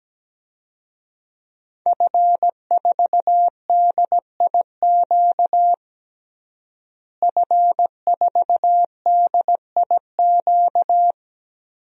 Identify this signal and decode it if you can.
Morse code, heard as a keyed tone, about 17 wpm, 700 Hz. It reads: F4DIQ F4DIQ
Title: DXCC-100 Call Signs - 5 Characters - Repeated 3x - 17wpm